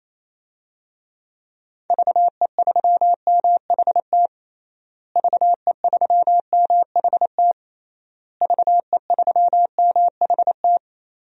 Morse code, heard as a keyed tone, about 28 wpm, 700 Hz. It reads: VE3MHT VE3MHT VE3MHT
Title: DXCC-100 Call Signs - 6 Characters - Repeated 3x - 28wpm